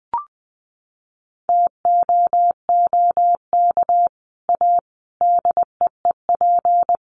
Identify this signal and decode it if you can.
Morse code, heard as a keyed tone, about 20 wpm, 700 Hz. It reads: TOOK A DEEP
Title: Sets of 5 Words Encore 20wpm